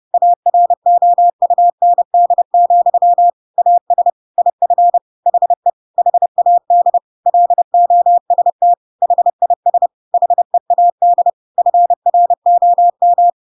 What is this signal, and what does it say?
Morse code, heard as a keyed tone, about 30 wpm, 700 Hz. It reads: AROUND, AS IF HE HAD LOST HIS HEAD FROM
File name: Pinocchio / Pinocchio ch8 30 30 ik1hge